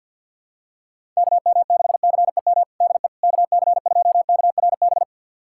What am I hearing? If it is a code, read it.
Morse code, heard as a keyed tone, about 51 wpm, 700 Hz. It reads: KM6CEM DE KC2KRB
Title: USA Callsign DE Callsign - 6 Characters - 50wpm